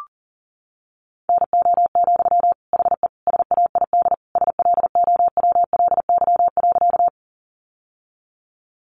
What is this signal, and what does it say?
Morse code, heard as a keyed tone, about 40 wpm, 700 Hz. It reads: NO, HE SAID SLOWLY.
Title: Sentences from Top 1100 Words 40wpm